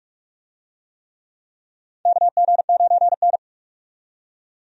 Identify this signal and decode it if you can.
Morse code, heard as a keyed tone, about 46 wpm, 700 Hz. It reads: KG9N
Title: USA Call Signs - 4 Characters - 45wpm